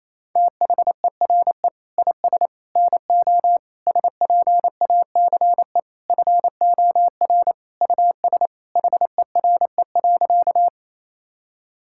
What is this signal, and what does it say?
Morse code, heard as a keyed tone, about 28 wpm, 700 Hz. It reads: THERE IS NO SPACE FOR US HERE.
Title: Sentences from Top 1000 Words - Review Part 5 - 28wpm